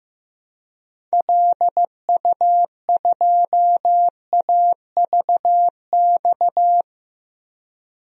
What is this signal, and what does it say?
Morse code, heard as a keyed tone, about 15 wpm, 700 Hz. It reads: LU2AVX
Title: DXCC-100 Call Signs - 6 Characters - 15wpm